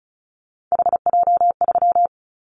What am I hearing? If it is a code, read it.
Morse code, heard as a keyed tone, about 34 wpm, 700 Hz. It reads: HJ3
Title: Callsign Prefixes for Top 100 DXCC - 35wpm